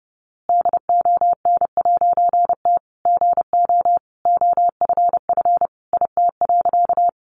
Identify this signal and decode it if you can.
Morse code, heard as a keyed tone, about 30 wpm, 700 Hz. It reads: DON'T GO OFF IT.
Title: A Sound Of Thunder 30wpm